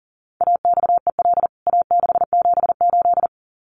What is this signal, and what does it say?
Morse code, heard as a keyed tone, about 40 wpm, 700 Hz. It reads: AXEL A678
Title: CWT Exchange 40wpm